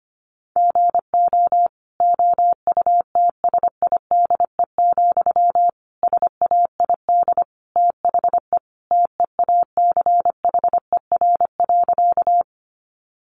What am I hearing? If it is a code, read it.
Morse code, heard as a keyed tone, about 25 wpm, 700 Hz. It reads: GO OUTSIDE, SAID THE TEACHER.
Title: Sentences from Top 900 Words 25wpm